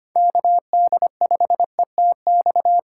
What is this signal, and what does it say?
Morse code, heard as a keyed tone, about 25 wpm, 700 Hz. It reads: KD5ETX
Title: USA Call Signs - 6 Characters - 6x Character Spacing - 25wpm